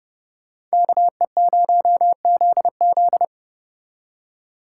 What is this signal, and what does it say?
Morse code, heard as a keyed tone, about 35 wpm, 700 Hz. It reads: KE0ZZ